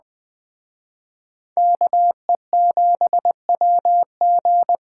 Morse code, heard as a keyed tone, about 20 wpm, 700 Hz. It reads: KE7WG
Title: Speed Racing - USA Call Signs 17wpm